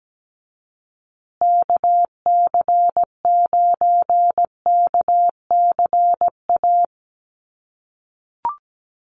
Morse code, heard as a keyed tone, about 17 wpm, 700 Hz. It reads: KC9KCA
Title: USA Call Signs Repeated 3x 17wpm